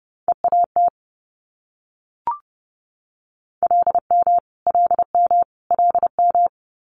Morse code, heard as a keyed tone, about 30 wpm, 700 Hz. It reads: EAT LM LM LM
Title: Multi-letter Phonemes Repeated 3x 30wpm